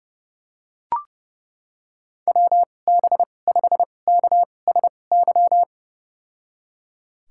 Morse code, heard as a keyed tone, about 32 wpm, spaced as for 22 wpm, 700 Hz. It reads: WB5KSY
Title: USA Call Signs - 6 Characters - 2x Character Spacing - 30wpm